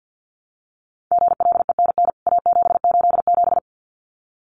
Morse code, heard as a keyed tone, about 48 wpm, 700 Hz. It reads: GLENN A787
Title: CWT Exchange 50wpm